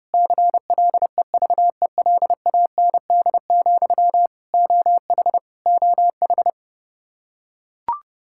Morse code, heard as a keyed tone, about 30 wpm, 700 Hz. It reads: CLEVELAND, OH OH
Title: K1USN SST QSO Practice 30wpm